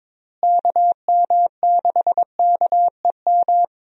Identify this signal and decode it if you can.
Morse code, heard as a keyed tone, about 22 wpm, 700 Hz. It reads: KM6KEM
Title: USA Call Signs - 6 Characters - Repeated 2x - 22wpm